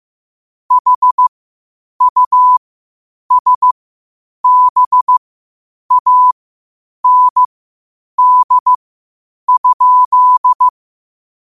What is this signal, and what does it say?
Morse code, heard as a keyed tone, about 15 wpm, spaced as for 8 wpm, 1000 Hz. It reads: HUSBAND?
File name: 8wpm